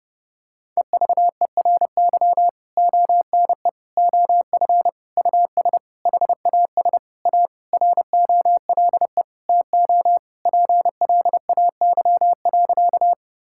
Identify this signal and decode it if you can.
Morse code, heard as a keyed tone, about 32 wpm, 700 Hz. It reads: EVERY ONE OF US HAS A ROLE TO PLAY.
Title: Sentences from Top 1000 Words - Review Part 6 - 30wpm